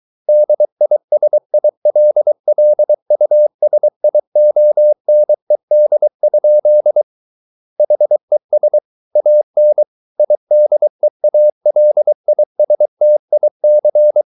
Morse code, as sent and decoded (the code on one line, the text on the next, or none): -.. .. ... .. .-.. .-.. ..- ... .. --- -. -.. ..--.. / .... ... / .- -. / .. -.. .- .-.. .. ... - .. -.-.
DISILLUSIONED? HES AN IDEALISTIC